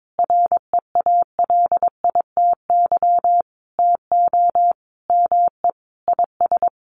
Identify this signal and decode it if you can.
Morse code, heard as a keyed tone, about 22 wpm, 700 Hz. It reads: REALITY TO ME IS
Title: Sentences from Top 1200 Words 22wpm